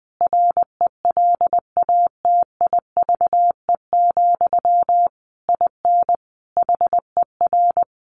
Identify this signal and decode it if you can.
Morse code, heard as a keyed tone, about 20 wpm, 700 Hz. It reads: RELATIVE, IN HER
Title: Sentences from Top 1400 Words 20wpm